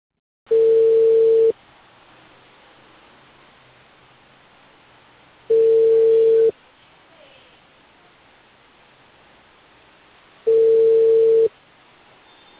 Ringback tone (China, 450 Hz).